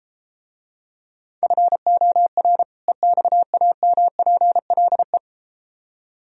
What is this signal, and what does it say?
Morse code, heard as a keyed tone, about 33 wpm, 700 Hz. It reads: FOR EXAMPLE